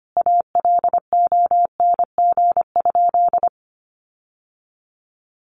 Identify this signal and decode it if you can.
Morse code, heard as a keyed tone, about 25 wpm, 700 Hz. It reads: ALONG?
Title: Sentences from Top 1300 Words 25wpm